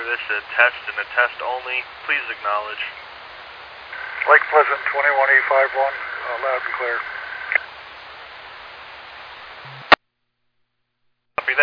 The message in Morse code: none

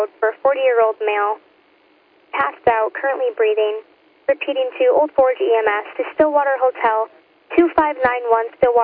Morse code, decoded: none

